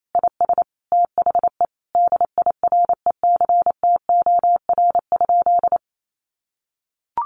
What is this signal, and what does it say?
Morse code, heard as a keyed tone, about 28 wpm, 700 Hz. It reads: IS THE DIRECTOR?
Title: Sentences from Top 1300 Words 28wpm